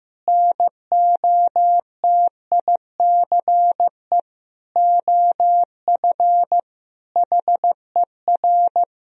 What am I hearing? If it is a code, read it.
Morse code, heard as a keyed tone, about 15 wpm, 700 Hz. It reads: NOTICE OF HER
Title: Sentences from Top 1000 Words - Review Part 2 - 15wpm